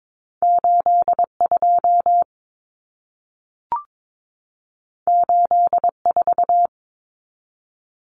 Morse code, heard as a keyed tone, about 22 wpm, 700 Hz. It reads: 82 84
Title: Numbers - 2-Digit 22wpm